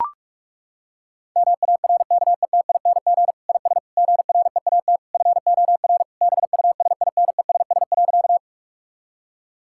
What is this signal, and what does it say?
Morse code, heard as a keyed tone, about 43 wpm, 700 Hz. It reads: MARKETING IS GREAT FOR BUSINESS.
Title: Sentences from Top 1000 Words - Review Part 5 - 45wpm